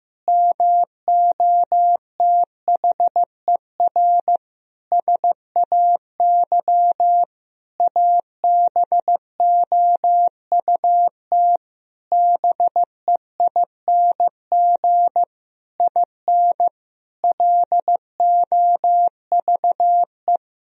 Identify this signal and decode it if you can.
Morse code, heard as a keyed tone, about 15 wpm, 700 Hz. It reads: MOTHER SAY ABOUT BEING IN LOVE
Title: Sentences from Top 1000 Words - Review Part 3 - 15wpm